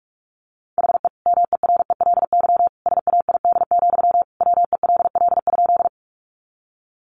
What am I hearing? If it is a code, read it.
Morse code, heard as a keyed tone, about 44 wpm, 700 Hz. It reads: HE MERELY SAID, WELL?
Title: Sentences from Top 1400 Words 45wpm